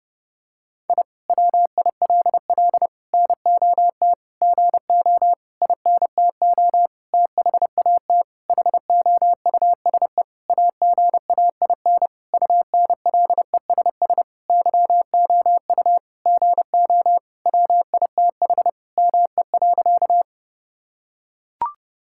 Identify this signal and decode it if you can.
Morse code, heard as a keyed tone, about 32 wpm, 700 Hz. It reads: I WILL NOT GO INTO THAT HOUSE AGAIN UNLESS YOU GO WITH ME.
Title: Sentences from Top 1000 Words - Review Part 6 - 30wpm